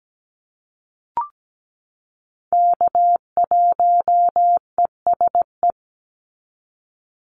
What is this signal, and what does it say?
Morse code, heard as a keyed tone, about 17 wpm, 700 Hz. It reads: K1ESE
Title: Call Signs - 17wpm